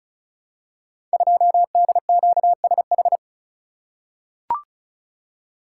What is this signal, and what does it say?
Morse code, heard as a keyed tone, about 35 wpm, 700 Hz. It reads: 2DQSH
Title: Code Groups - 5 Letter:Numbers - 35wpm